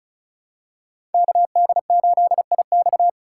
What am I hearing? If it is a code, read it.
Morse code, heard as a keyed tone, about 34 wpm, 700 Hz. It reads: KD8IX
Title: USA Call Signs - 5 Characters - 2x Character Spacing - 35wpm